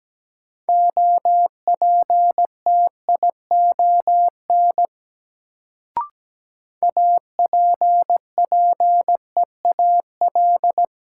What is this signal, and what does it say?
Morse code, heard as a keyed tone, about 17 wpm, 700 Hz. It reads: OPTION APPEAL
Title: Sentences from Top 1300 Words - New Word Drill - 17wpm